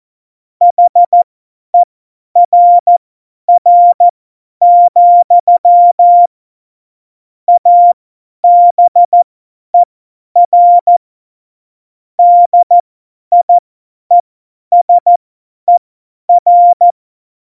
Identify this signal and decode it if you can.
Morse code, heard as a keyed tone, about 14 wpm, spaced as for 10 wpm, 700 Hz. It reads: HERR, ABER DIESER
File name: Morsekurs / Witze / Witz 23. Juli 2019, 50 F70